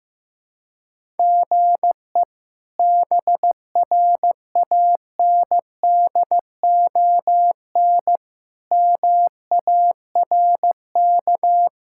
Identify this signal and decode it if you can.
Morse code, heard as a keyed tone, about 15 wpm, 700 Hz. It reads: GE BRANDON MARK
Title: K1USN SST QSO Practice - 15wpm